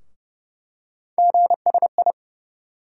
Morse code, heard as a keyed tone, about 30 wpm, 700 Hz. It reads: GSI
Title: USA Call Sign Postfixes - 3 Letters 30wpm